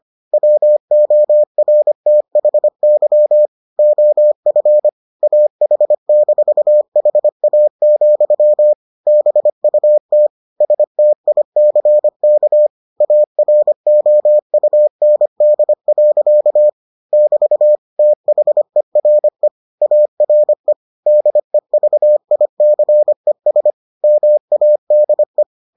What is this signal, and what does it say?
Morse code, heard as a keyed tone, about 25 wpm, 600 Hz. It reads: WORTHY OF AH-HA, BUT STICK AROUND. <BT> THERE ARE DEVICES MADE